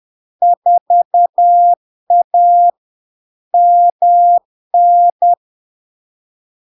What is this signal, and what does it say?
Morse code, heard as a keyed tone, about 10 wpm, 700 Hz. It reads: 4A MN
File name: LARK-FD-2022-Log-in-CW-10wpm-700-Hz